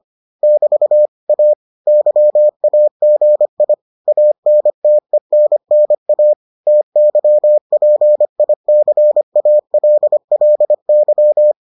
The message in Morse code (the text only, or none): <BT> A YAGI ANTENNA TYPICALLY